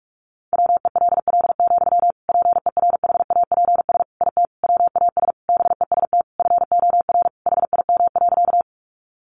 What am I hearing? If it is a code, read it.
Morse code, heard as a keyed tone, about 44 wpm, 700 Hz. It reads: WELL, PERHAPS IT WAS BEST FOR HIM.
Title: Sentences from Top 500 Words 45wpm